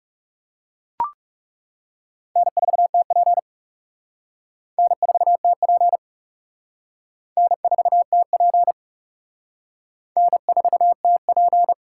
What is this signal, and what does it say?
Morse code, heard as a keyed tone, about 37 wpm, 700 Hz. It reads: N4TP N4TP N4TP N4TP
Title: Speed Racing - USA 1x2 Call Signs 30wpm